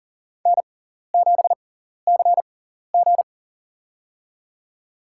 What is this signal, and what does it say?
Morse code, heard as a keyed tone, about 40 wpm, spaced as for 14 wpm, 690 Hz. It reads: N7CG
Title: USA Call Signs - 6x Character Spacing - 40wpm